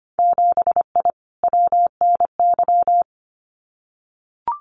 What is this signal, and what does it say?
Morse code, heard as a keyed tone, about 25 wpm, 700 Hz. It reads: 7I WNY
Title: Winter Field Day Exchanges Encore 25wpm